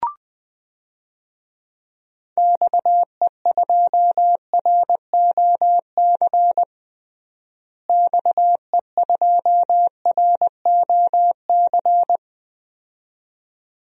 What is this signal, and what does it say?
Morse code, heard as a keyed tone, about 20 wpm, 700 Hz. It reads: XE2ROC XE2ROC
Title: DXCC-100 Call Signs - 6 Characters - Repeated 2x - 20wpm